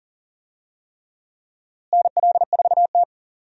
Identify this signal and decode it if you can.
Morse code, heard as a keyed tone, about 40 wpm, 700 Hz. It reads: NL4T
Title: USA Call Signs - 4 Characters - Repeated 3x - 40wpm